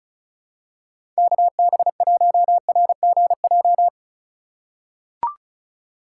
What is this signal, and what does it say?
Morse code, heard as a keyed tone, about 34 wpm, 700 Hz. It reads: KB1RGJ
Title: USA Call Signs - 6 Characters - 4x Character Spacing - 35wpm